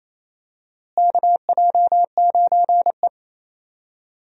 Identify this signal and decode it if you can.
Morse code, heard as a keyed tone, about 28 wpm, 700 Hz. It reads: KJ9E
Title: USA Call Signs - 4 Characters - Repeated 3x - 28wpm